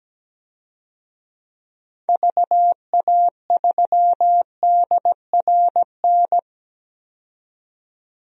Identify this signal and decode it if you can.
Morse code, heard as a keyed tone, about 17 wpm, 700 Hz. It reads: VA3DRN
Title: DXCC-100 Call Signs - 6 Characters - 17wpm